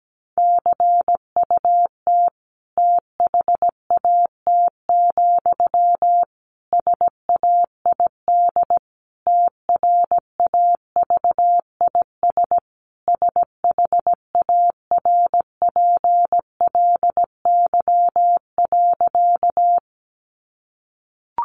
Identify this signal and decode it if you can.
Morse code, heard as a keyed tone, about 17 wpm, 700 Hz. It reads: CUT THAT, SAID TRAVIS SHARPLY.
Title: A Sound Of Thunder 17wpm